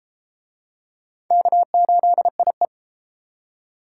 Morse code, heard as a keyed tone, about 31 wpm, 700 Hz. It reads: K8IE